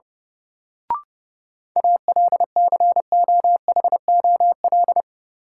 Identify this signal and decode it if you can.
Morse code, heard as a keyed tone, about 30 wpm, 700 Hz. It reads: ALCOHOL
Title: Sentences from Top 1400 Words - New Word Drill Alt - 30wpm